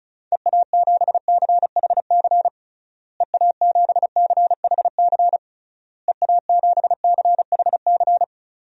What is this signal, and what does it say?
Morse code, heard as a keyed tone, about 36 wpm, 700 Hz. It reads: EA7CHC EA7CHC EA7CHC
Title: DXCC-100 Call Signs - 6 Characters - Repeated 3x - 35wpm